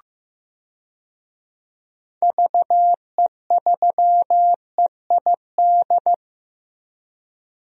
Morse code, heard as a keyed tone, about 15 wpm, 700 Hz. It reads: VE3EID
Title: DXCC-100 Call Signs - 6 Characters - Repeated 3x - 15wpm